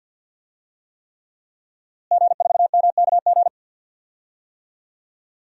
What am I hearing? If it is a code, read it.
Morse code, heard as a keyed tone, about 52 wpm, 700 Hz. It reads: G4MKG
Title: DXCC-100 Call Signs - 5 Characters - Repeated 3x - 50wpm